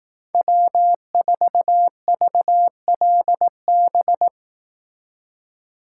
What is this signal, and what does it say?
Morse code, heard as a keyed tone, about 18 wpm, 700 Hz. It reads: W4VLB
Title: Speed Racing - USA 1x3 Call Signs 15wpm